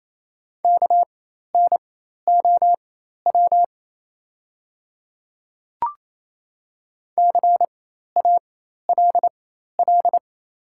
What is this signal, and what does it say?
Morse code, heard as a keyed tone, about 27 wpm, spaced as for 13 wpm, 690 Hz. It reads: KNOW CALL
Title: 4x Character Spacing - Top 100 Words - 28wpm